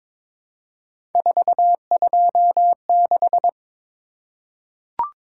Morse code, heard as a keyed tone, about 22 wpm, 700 Hz. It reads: 426
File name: practice / 22wpm / Three Letter-Number Combinations - 22wpm